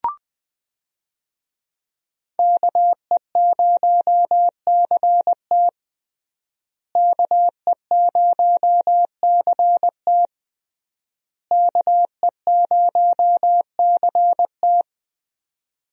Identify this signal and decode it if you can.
Morse code, heard as a keyed tone, about 20 wpm, 700 Hz. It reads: KE0CT KE0CT KE0CT
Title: USA Call Signs - 5 Characters - Repeated 3x - 20wpm